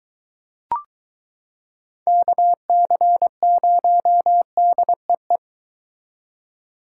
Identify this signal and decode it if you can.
Morse code, heard as a keyed tone, about 22 wpm, 700 Hz. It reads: KC0DEE